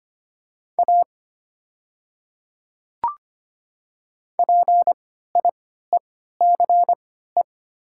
Morse code, heard as a keyed tone, about 25 wpm, spaced as for 14 wpm, 700 Hz. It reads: A PIECE
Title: 3x Character Spacing - Top 500 Words - 25wpm